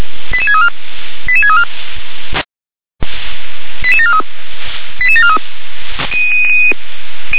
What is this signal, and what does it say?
Five-tone selective call (ZVEI), heard as quick runs of five tones, tones of about 70 ms. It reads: 88634 88634 88743 88743
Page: ZVEI